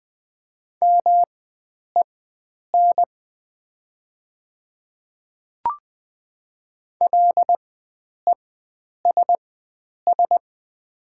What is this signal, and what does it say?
Morse code, heard as a keyed tone, about 20 wpm, spaced as for 9 wpm, 700 Hz. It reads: MEN LESS
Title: 4x Character Spacing - Top 500 Words - 20wpm